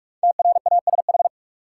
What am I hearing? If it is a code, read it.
Morse code, heard as a keyed tone, about 46 wpm, 700 Hz. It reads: TRASH
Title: Five Letter Words - 45wpm